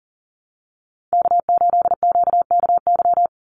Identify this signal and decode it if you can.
Morse code, heard as a keyed tone, about 40 wpm, 700 Hz. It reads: K8QKY
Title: Call Signs - 40wpm